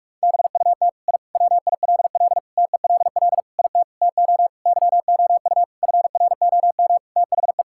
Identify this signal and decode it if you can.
Morse code, heard as a keyed tone, about 46 wpm, 700 Hz. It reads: BUT I WILL TELL IT TO YOU FROM THE
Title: Sentences from Top 1000 Words - Review Part 5 - 45wpm